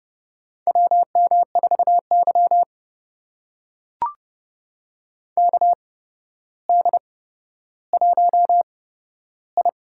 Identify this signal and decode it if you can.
Morse code, heard as a keyed tone, about 30 wpm, spaced as for 13 wpm, 700 Hz. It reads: WM4Y KD1I
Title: USA Call Signs - 4 Characters - 8x Character Spacing - 30wpm